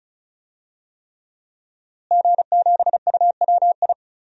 Morse code, heard as a keyed tone, about 36 wpm, 700 Hz. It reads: G7UWI
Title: DXCC-100 Call Signs - 5 Characters - Repeated 3x - 35wpm